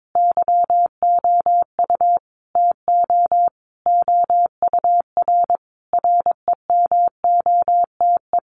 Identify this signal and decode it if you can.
Morse code, heard as a keyed tone, about 22 wpm, 700 Hz. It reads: YOU TO OUR REMOTE
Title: Sentences from Top 1500 Words 22wpm